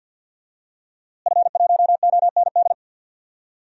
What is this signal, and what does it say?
Morse code, heard as a keyed tone, about 46 wpm, 700 Hz. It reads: F1OND
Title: DXCC-100 Call Signs - 5 Characters - Repeated 2x - 50wpm